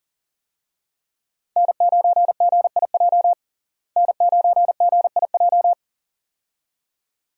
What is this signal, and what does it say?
Morse code, heard as a keyed tone, about 40 wpm, 700 Hz. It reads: N9GIJ N9GIJ
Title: USA Call Signs - 5 Characters - Repeated 2x - 40wpm